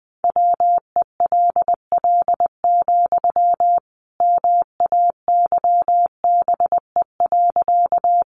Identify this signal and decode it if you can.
Morse code, heard as a keyed tone, about 20 wpm, 700 Hz. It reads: WELL, MAYBE.
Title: Sentences from Top 1000 Words 20wpm